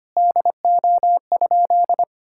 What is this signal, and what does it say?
Morse code, heard as a keyed tone, about 25 wpm, 700 Hz. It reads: DO?
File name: practice / 25wpm / Sentences with 3-Word Phrases - 25wpm